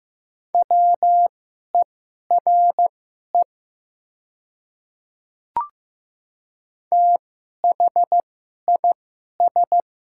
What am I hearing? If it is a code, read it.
Morse code, heard as a keyed tone, about 15 wpm, spaced as for 11 wpm, 700 Hz. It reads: WERE THIS
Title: 2x Character Spacing - Top 100 Words - 15wpm